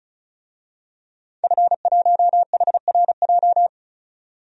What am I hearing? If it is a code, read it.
Morse code, heard as a keyed tone, about 36 wpm, 700 Hz. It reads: F1HRJ
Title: DXCC-100 Call Signs - 5 Characters - 35wpm